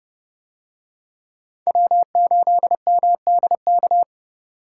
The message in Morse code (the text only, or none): W8MDK